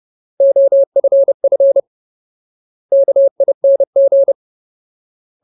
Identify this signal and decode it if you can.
Morse code, heard as a keyed tone, about 30 wpm, 550 Hz. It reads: OFF KING
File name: Ultimate Word List / 30WPM Ultimate